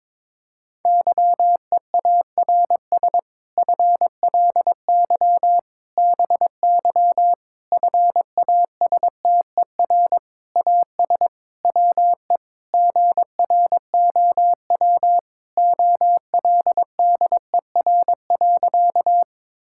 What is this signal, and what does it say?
Morse code, heard as a keyed tone, about 22 wpm, 700 Hz. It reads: YEARS FLY BY FASTER AS WE GROW OLDER.
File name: Difficult Sentences with Hints - 22wpm